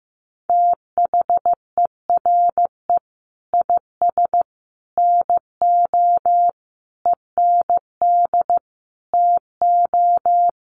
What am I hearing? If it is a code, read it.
Morse code, heard as a keyed tone, about 15 wpm, 700 Hz. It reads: THERE IS NO END TO
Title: Sentences from Top 1200 Words 15wpm